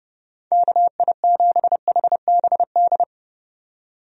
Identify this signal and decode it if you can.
Morse code, heard as a keyed tone, about 30 wpm, 700 Hz. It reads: KI7HBD
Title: USA Call Signs - 6 Characters - 30wpm